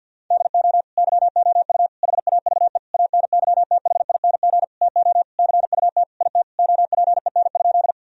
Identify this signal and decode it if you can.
Morse code, heard as a keyed tone, about 53 wpm, 700 Hz. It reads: DO YOU HAVE ANYTHING TO CUT IT OPEN?
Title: Sentences from Top 1000 Words - Review Part 1 - 50wpm